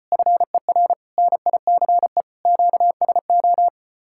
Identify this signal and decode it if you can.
Morse code, heard as a keyed tone, about 32 wpm, 700 Hz. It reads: FER NICE QSO